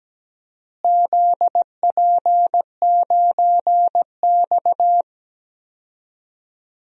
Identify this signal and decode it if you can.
Morse code, heard as a keyed tone, about 17 wpm, 700 Hz. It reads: ZP9X
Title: DXCC-100 Call Signs - 4 Characters - Repeated 2x - 17wpm